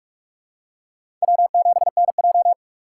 Morse code, heard as a keyed tone, about 46 wpm, 700 Hz. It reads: W7NJ
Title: USA Call Signs - 4 Characters - 3x Character Spacing - 45wpm